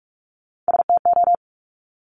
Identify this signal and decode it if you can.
Morse code, heard as a keyed tone, about 46 wpm, 700 Hz. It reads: STO
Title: Top Three Letter Combinations in Words 45wpm